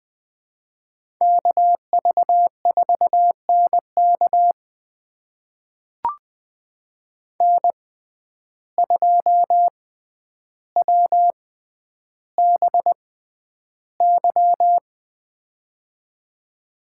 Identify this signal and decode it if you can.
Morse code, heard as a keyed tone, about 20 wpm, spaced as for 10 wpm, 700 Hz. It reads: KV4NK N2WBY
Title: USA Call Signs - 5 Characters - 6x Character Spacing - 20wpm